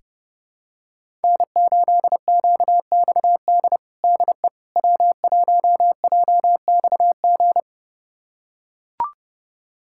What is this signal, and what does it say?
Morse code, heard as a keyed tone, about 30 wpm, 700 Hz. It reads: N8QXD DE W1JXG
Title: USA Callsign DE Callsign 30wpm